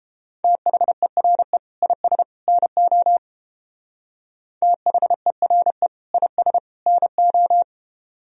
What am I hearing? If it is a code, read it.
Morse code, heard as a keyed tone, about 29 wpm, 700 Hz. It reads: THERE IS NO THERE IS NO